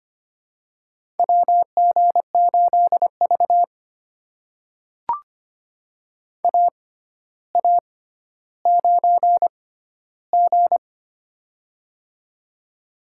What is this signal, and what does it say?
Morse code, heard as a keyed tone, about 25 wpm, spaced as for 13 wpm, 700 Hz. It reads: WG8V AA9G
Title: USA Call Signs - 4 Characters - 6x Character Spacing - 25wpm